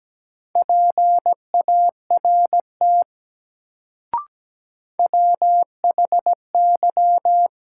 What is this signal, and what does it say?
Morse code, heard as a keyed tone, about 17 wpm, 700 Hz. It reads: PART WHY
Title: Sentences from Top 200 Words - New Word Drill - 17wpm